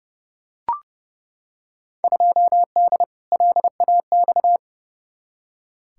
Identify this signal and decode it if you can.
Morse code, heard as a keyed tone, about 30 wpm, 700 Hz. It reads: 2D LAX
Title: Field Day Exchanges - 30wpm